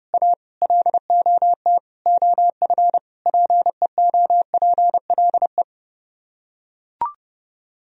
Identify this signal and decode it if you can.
Morse code, heard as a keyed tone, about 30 wpm, 700 Hz. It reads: A LOT OF PEOPLE